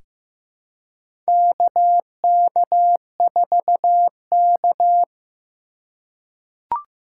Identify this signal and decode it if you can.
Morse code, heard as a keyed tone, about 15 wpm, 700 Hz. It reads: KK4K
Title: USA Call Signs - 4 Characters - 4x Character Spacing - 15wpm